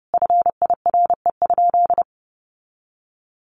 Morse code, heard as a keyed tone, about 30 wpm, 700 Hz. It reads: FIRE?
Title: Sentences from Top 1000 Words 30wpm